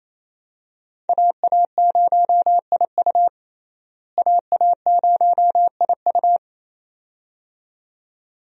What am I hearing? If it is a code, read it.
Morse code, heard as a keyed tone, about 28 wpm, 700 Hz. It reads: AA0IU AA0IU
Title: USA Call Signs - 5 Characters - Repeated 2x - 28wpm